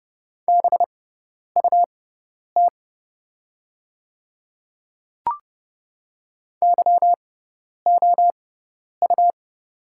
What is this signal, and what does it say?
Morse code, heard as a keyed tone, about 31 wpm, spaced as for 10 wpm, 700 Hz. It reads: BUT YOU
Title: 6x Character Spacing - Top 100 Words - 30wpm